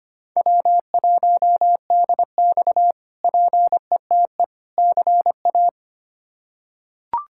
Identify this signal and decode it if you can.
Morse code, heard as a keyed tone, about 25 wpm, 700 Hz. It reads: W1DX PETE CA
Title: K1USN SST QSO Practice - 25wpm